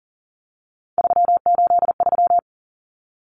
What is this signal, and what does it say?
Morse code, heard as a keyed tone, about 40 wpm, 700 Hz. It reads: 383
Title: Numbers - 3-Digit 40wpm